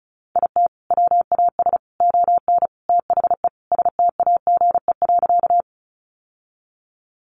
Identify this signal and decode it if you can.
Morse code, heard as a keyed tone, about 34 wpm, 700 Hz. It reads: IT WAS ON THE STAGE.
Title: Sentences from Top 1200 Words 35wpm